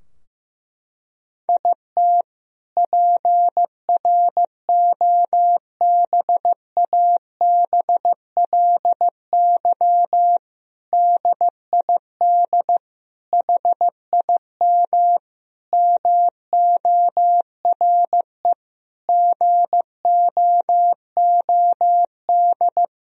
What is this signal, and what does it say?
Morse code, heard as a keyed tone, about 15 wpm, 700 Hz. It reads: IT PROBABLY DID HIM MORE GOOD
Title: Sentences from Top 1000 Words - Review Part 4 - 15wpm